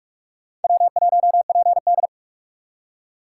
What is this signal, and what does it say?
Morse code, heard as a keyed tone, about 56 wpm, 700 Hz. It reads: W1PD